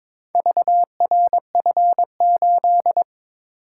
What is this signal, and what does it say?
Morse code, heard as a keyed tone, about 22 wpm, 690 Hz. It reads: VRF8
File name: practice / 22wpm / Four Letter-Number Combinations - 22wpm